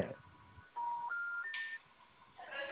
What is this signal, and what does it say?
Special information tone.
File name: out- -2027- - - .0